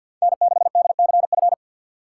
Morse code, heard as a keyed tone, about 53 wpm, 700 Hz. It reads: N6DCF
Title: USA Call Signs - 5 Characters - 6x Character Spacing - 50wpm